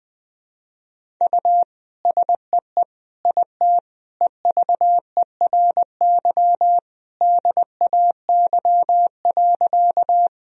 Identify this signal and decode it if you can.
Morse code, heard as a keyed tone, about 20 wpm, 700 Hz. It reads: U SEE IT EVERY DAY.
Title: Sentences from Top 200 Words and Abbreviations 20wpm